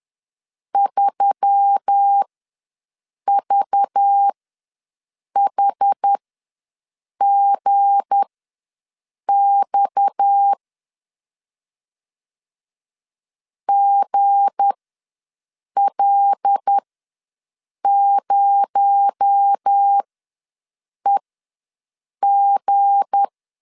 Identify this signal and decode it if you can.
Morse code, heard as a keyed tone, about 10 wpm, spaced as for 6 wpm, 800 Hz. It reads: 3VHGX GL0EG
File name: LEZIONI / 7 GF9Z / 7 G ws 10wpm 3843